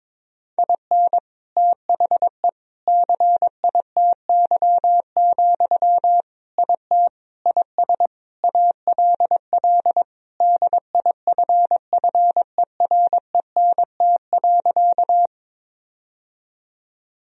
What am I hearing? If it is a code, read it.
Morse code, heard as a keyed tone, about 22 wpm, 700 Hz. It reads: IN THE CITY, IT IS ALL DIFFERENT.
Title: Sentences from Top 1000 Words - Review Part 3 - 22wpm